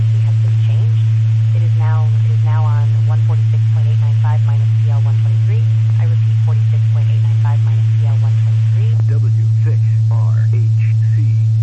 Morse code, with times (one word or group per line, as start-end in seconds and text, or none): none